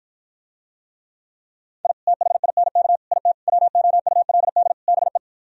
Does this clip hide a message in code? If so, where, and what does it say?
Yes: Morse code from 1.8 to 5.2 s, heard as a keyed tone, about 56 wpm, 700 Hz. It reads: I THINK IT WOULD BE